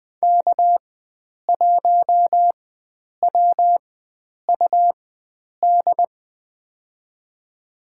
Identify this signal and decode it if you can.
Morse code, heard as a keyed tone, about 20 wpm, spaced as for 9 wpm, 700 Hz. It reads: K1WUD
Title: USA Call Signs - 5 Characters - 4x Character Spacing - 20wpm